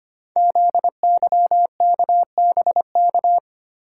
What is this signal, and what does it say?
Morse code, heard as a keyed tone, about 25 wpm, 700 Hz. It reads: ZYKBK
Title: Code Groups - 5 Letters - 25wpm